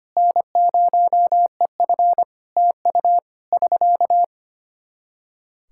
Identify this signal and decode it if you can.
Morse code, heard as a keyed tone, about 25 wpm, 700 Hz. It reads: N0EF TU <SK>
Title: Practice QSO - Exchange 4 - Ending - 25wpm